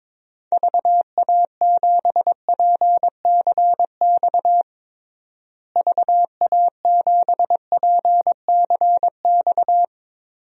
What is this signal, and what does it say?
Morse code, heard as a keyed tone, about 22 wpm, 700 Hz. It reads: VA7PCX VA7PCX
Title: DXCC-100 Call Signs - 6 Characters - Repeated 3x - 22wpm